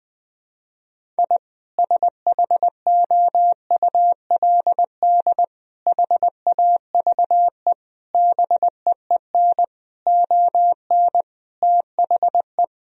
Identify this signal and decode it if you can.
Morse code, heard as a keyed tone, about 20 wpm, 700 Hz. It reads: I SHOULD HAVE BEEN ON THE
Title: Sentences from Top 1000 Words - Review Part 1 - 20wpm